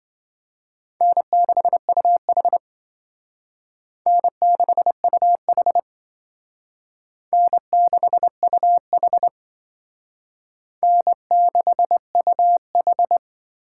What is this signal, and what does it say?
Morse code, heard as a keyed tone, about 26 wpm, 700 Hz. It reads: N6UH N6UH N6UH N6UH